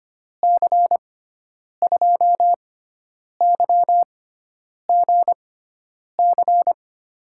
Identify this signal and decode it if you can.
Morse code, heard as a keyed tone, about 25 wpm, spaced as for 9 wpm, 700 Hz. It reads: C2YGC
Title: USA Call Signs - 6 Characters - 6x Character Spacing - 25wpm